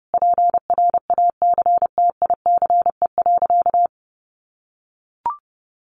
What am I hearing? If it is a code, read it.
Morse code, heard as a keyed tone, about 30 wpm, 700 Hz. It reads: PRACTICE.
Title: Easy Phrases 30wpm